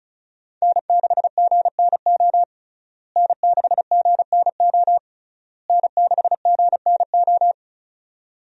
Morse code, heard as a keyed tone, about 36 wpm, 700 Hz. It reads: N6GNO N6GNO N6GNO
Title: USA Call Signs - 5 Characters - Repeated 3x - 35wpm